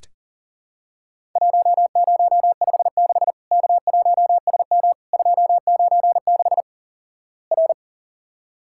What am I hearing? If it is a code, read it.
Morse code, heard as a keyed tone, about 43 wpm, 700 Hz. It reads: 1056 K1SM 296 R
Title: Worked All Europe DX Contest - QTC Exchange - 40wpm